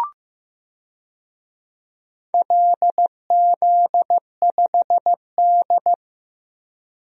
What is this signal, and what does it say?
Morse code, heard as a keyed tone, about 16 wpm, 700 Hz. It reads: LZ5D
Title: DXCC-100 Call Signs - 4 Characters - Repeated 2x - 15wpm